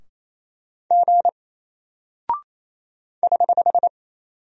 Morse code, heard as a keyed tone, about 27 wpm, 700 Hz. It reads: G <HH>
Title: Single Character (Letter:Number:Punctuation:Prosign) Alt 28wpm